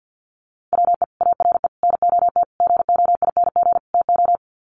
Morse code, heard as a keyed tone, about 50 wpm, 700 Hz. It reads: WE ARE NOT GOING TO
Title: Sets of 5 Words 50wpm